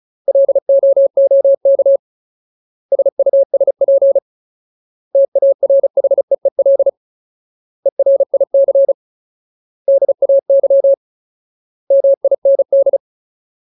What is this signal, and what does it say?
Morse code, heard as a keyed tone, about 34 wpm, 550 Hz. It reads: LOOK SUSP TARHEEL ERIC DAY MIND
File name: Ultimate Word List / 35WPM Ultimate